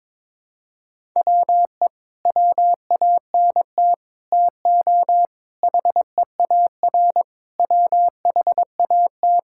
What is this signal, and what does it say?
Morse code, heard as a keyed tone, about 22 wpm, 700 Hz. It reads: WE WANT TO HEAR WHAT
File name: practice / 22wpm / Sets of 8 Words - 22wpm